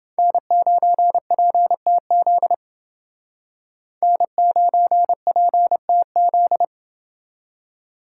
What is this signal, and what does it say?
Morse code, heard as a keyed tone, about 29 wpm, 700 Hz. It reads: N9PTZ N9PTZ